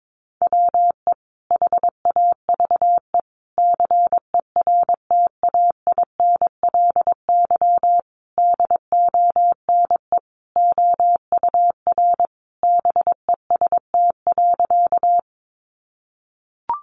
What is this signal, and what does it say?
Morse code, heard as a keyed tone, about 22 wpm, 700 Hz. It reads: WE HAVE CERTAINLY DONE OUR BEST.
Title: Sentences from Top 900 Words 22wpm